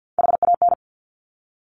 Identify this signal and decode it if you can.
Morse code, heard as a keyed tone, about 50 wpm, 700 Hz. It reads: HAN
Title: Top Three Letter Combinations in Words 50wpm